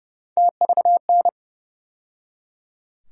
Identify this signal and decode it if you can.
Morse code, heard as a keyed tone, about 31 wpm, 700 Hz. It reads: TVN
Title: USA Call Sign Postfixes - 3 Letters 30wpm